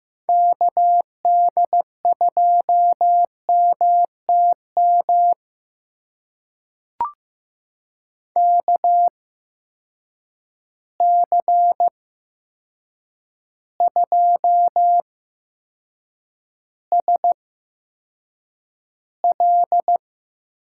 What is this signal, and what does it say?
Morse code, heard as a keyed tone, about 15 wpm, spaced as for 7 wpm, 700 Hz. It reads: KD2MTM KC2SL
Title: USA Call Signs - 6 Characters - 8x Character Spacing - 15wpm